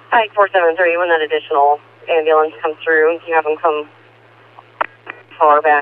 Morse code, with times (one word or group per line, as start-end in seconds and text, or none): none